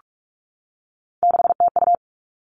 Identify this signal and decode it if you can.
Morse code, heard as a keyed tone, about 46 wpm, 700 Hz. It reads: 6TU
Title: Three Challenging Letter-Number Combinations 45wpm